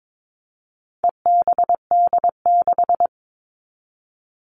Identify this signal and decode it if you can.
Morse code, heard as a keyed tone, about 22 wpm, 700 Hz. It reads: EBD6
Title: All Cut Numbers - 4-Digit 22wpm